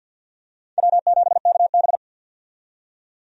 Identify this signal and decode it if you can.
Morse code, heard as a keyed tone, about 46 wpm, 700 Hz. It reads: W7KB
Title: USA Call Signs - 4 Characters - 50wpm